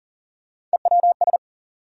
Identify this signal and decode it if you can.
Morse code, heard as a keyed tone, about 40 wpm, 700 Hz. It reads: EWS
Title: USA Call Sign Postfixes - 3 Letters 40wpm